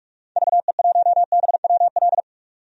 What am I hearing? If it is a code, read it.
Morse code, heard as a keyed tone, about 46 wpm, 700 Hz. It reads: VE1BWL
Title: DXCC-100 Call Signs - 6 Characters - 45wpm